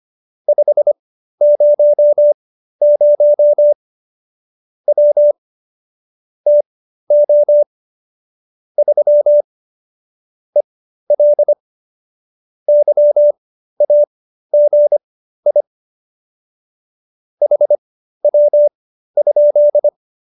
Morse code, heard as a keyed tone, about 25 wpm, spaced as for 13 wpm, 590 Hz. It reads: 500 W TO 3 EL YAGI HW?
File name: qso208b 13